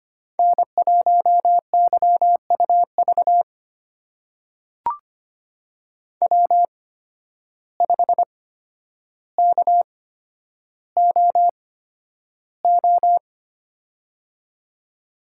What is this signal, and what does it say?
Morse code, heard as a keyed tone, about 25 wpm, spaced as for 11 wpm, 700 Hz. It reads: N1YUV W5KOO
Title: USA Call Signs - 5 Characters - 8x Character Spacing - 25wpm